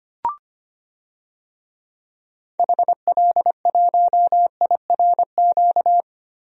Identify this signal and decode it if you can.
Morse code, heard as a keyed tone, about 25 wpm, 700 Hz. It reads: HL1IRQ
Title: DXCC-100 Call Signs - 6 Characters - 25wpm